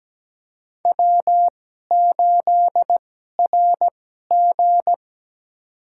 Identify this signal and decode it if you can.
Morse code, heard as a keyed tone, about 17 wpm, spaced as for 12 wpm, 700 Hz. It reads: W8RG
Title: USA Call Signs - 4 Characters - 2x Character Spacing - 17wpm